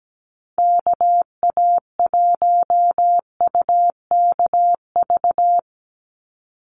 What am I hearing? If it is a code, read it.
Morse code, heard as a keyed tone, about 17 wpm, 700 Hz. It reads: KA1UKV
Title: USA Call Signs 17wpm